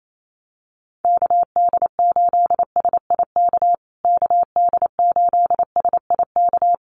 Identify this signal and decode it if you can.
Morse code, heard as a keyed tone, about 28 wpm, 700 Hz. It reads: KD8SIK KD8SIK
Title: USA Call Signs Repeated 3x 28wpm